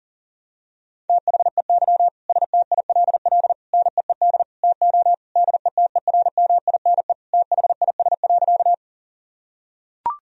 Morse code, heard as a keyed tone, about 37 wpm, 700 Hz. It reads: THEY STILL NEED TO DETERMINE THIS.
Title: Sentences from Top 1000 Words - Review Part 4 - 40wpm